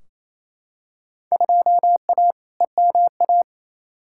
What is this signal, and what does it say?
Morse code, heard as a keyed tone, about 28 wpm, 700 Hz. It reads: 2A EMA
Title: Field Day Exchanges Encore - 28wpm